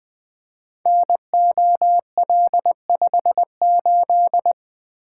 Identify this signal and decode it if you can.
Morse code, heard as a keyed tone, about 21 wpm, 700 Hz. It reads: NOL58
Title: Code Groups - 5 Letter:Numbers - 20wpm